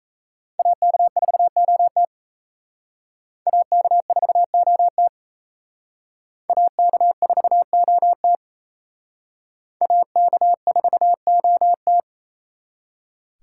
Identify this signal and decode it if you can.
Morse code, heard as a keyed tone, about 34 wpm, 700 Hz. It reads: AK4OT AK4OT AK4OT AK4OT
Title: Speed Racing - USA 2x2 Call Signs 28wpm